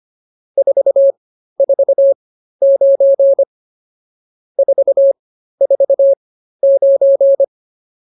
Morse code, heard as a keyed tone, about 24 wpm, spaced as for 13 wpm, 550 Hz. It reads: 449 449